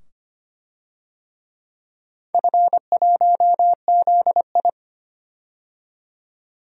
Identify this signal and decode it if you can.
Morse code, heard as a keyed tone, about 25 wpm, 700 Hz. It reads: F1ZI
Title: DXCC-100 Call Signs - 4 Characters - 25wpm